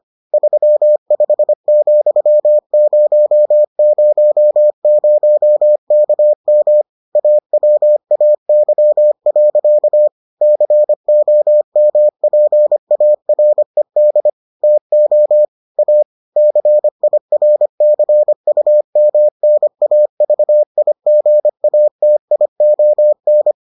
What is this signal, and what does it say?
Morse code, heard as a keyed tone, about 25 wpm, 600 Hz. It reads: 35,000KM AWAY. COMPARED TO A CIRCUMNAVIGATION